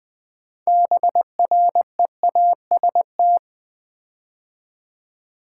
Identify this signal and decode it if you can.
Morse code, heard as a keyed tone, about 20 wpm, 700 Hz. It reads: BREAST
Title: 6 Letter Words - First 3 Letters Given - 20wpm